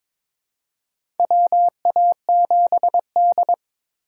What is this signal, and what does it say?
Morse code, heard as a keyed tone, about 23 wpm, 700 Hz. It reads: WA7D
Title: USA Call Signs - 4 Characters - 4x Character Spacing - 22wpm